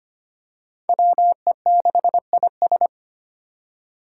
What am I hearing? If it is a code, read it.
Morse code, heard as a keyed tone, about 25 wpm, 700 Hz. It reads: WE6IS
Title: Five Challenging Letter-Number Combinations 25wpm